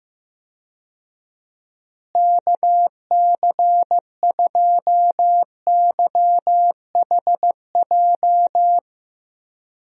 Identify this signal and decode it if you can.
Morse code, heard as a keyed tone, about 15 wpm, 700 Hz. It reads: KC2YHJ
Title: USA Call Signs - 6 Characters - Repeated 3x - 15wpm